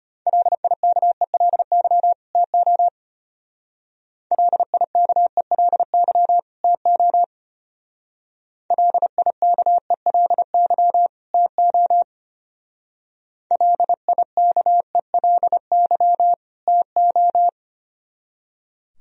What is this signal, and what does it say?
Morse code, heard as a keyed tone, about 31 wpm, 700 Hz. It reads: LIKELY TO LIKELY TO LIKELY TO LIKELY TO
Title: Speed-Racing - Sets of 2 Words 25wpm